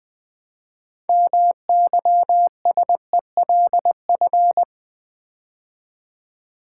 Morse code, heard as a keyed tone, about 20 wpm, 700 Hz. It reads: MYSELF
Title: 6 Letter Words - First 3 Letters Given - 20wpm